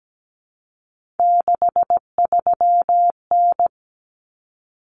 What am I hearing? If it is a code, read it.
Morse code, heard as a keyed tone, about 17 wpm, 700 Hz. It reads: 63N
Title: Common Cut Numbers - 3-Digit 17wpm